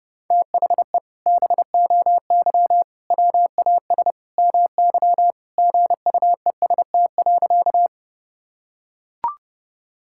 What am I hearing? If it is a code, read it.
Morse code, heard as a keyed tone, about 30 wpm, 700 Hz. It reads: THE BOY WAS MY GUEST.
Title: Sentences from Top 1600 Words 30wpm